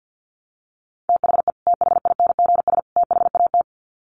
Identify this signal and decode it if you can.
Morse code, heard as a keyed tone, about 50 wpm, 700 Hz. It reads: THE THINGS THAT
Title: Sets of 3 Words 50wpm